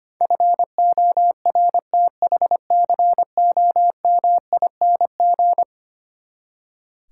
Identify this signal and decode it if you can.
Morse code, heard as a keyed tone, about 25 wpm, 700 Hz. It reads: FORTHCOMING
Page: morse-code-ninja-practice-25wpm